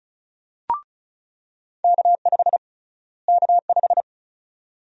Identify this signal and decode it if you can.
Morse code, heard as a keyed tone, about 36 wpm, 700 Hz. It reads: K5 K5
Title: USA Call Sign Prefixes - Repeated 2x - 35wpm